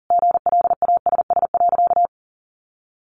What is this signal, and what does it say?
Morse code, heard as a keyed tone, about 40 wpm, 700 Hz. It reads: GLASS.
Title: Sentences from Top 1400 Words 40wpm